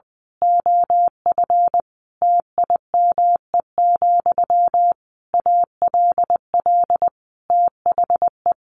Morse code, heard as a keyed tone, about 20 wpm, 700 Hz. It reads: OF TIME, ALL THE
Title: A Sound Of Thunder 20wpm